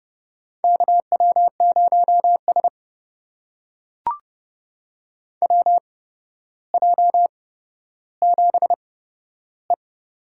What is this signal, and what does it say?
Morse code, heard as a keyed tone, about 30 wpm, spaced as for 13 wpm, 700 Hz. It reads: KW0S WJ7E
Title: USA Call Signs - 8x Character Spacing - 30wpm